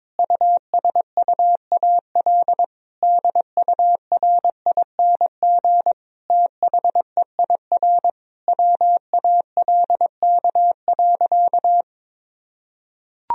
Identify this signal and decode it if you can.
Morse code, heard as a keyed tone, about 22 wpm, 700 Hz. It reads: USUAL DURING THEIR WALK.
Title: Sentences from Top 1000 Words - Review Part 2 - 22wpm